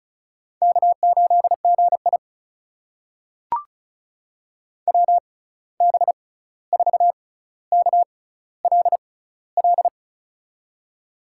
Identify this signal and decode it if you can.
Morse code, heard as a keyed tone, about 36 wpm, spaced as for 16 wpm, 700 Hz. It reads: K8GI WB4KLL
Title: USA Call Signs - 6x Character Spacing - 35wpm